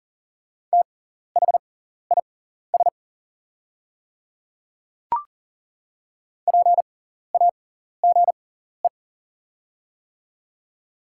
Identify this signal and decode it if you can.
Morse code, heard as a keyed tone, about 40 wpm, spaced as for 14 wpm, 700 Hz. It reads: THIS PAGE
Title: 6x Character Spacing - Top 300 Words - 40wpm